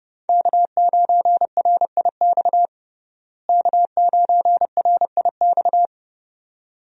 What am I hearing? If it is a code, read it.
Morse code, heard as a keyed tone, about 32 wpm, 700 Hz. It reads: K9RIX K9RIX
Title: USA Call Signs - 5 Characters - Repeated 2x - 30wpm